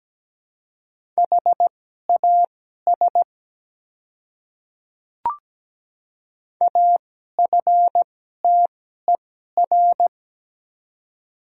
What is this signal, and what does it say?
Morse code, heard as a keyed tone, about 17 wpm, spaced as for 12 wpm, 700 Hz. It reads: HAS AFTER
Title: 2x Character Spacing - Top 200 Words - 17wpm